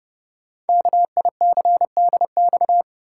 Morse code, heard as a keyed tone, about 29 wpm, 700 Hz. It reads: KICDX
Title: Code Groups - 5 Letter:Numbers - 30wpm